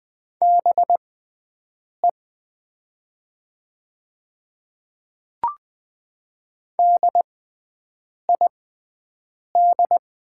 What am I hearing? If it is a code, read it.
Morse code, heard as a keyed tone, about 20 wpm, spaced as for 7 wpm, 700 Hz. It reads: BE DID
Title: 6x Character Spacing - Top 100 Words - 20wpm